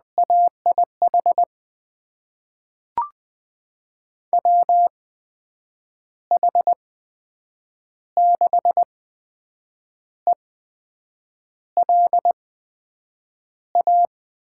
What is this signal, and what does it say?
Morse code, heard as a keyed tone, about 20 wpm, spaced as for 7 wpm, 700 Hz. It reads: AIH WH6ELA
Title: USA Call Signs - 6 Characters - 8x Character Spacing - 20wpm